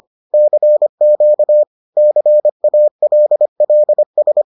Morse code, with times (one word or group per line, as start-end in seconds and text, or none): 0.3-1.6 CQ
2.0-4.4 CALLS